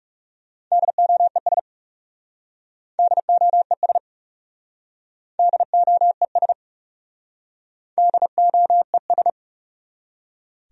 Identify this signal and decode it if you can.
Morse code, heard as a keyed tone, about 40 wpm, 700 Hz. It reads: DOES DOES DOES DOES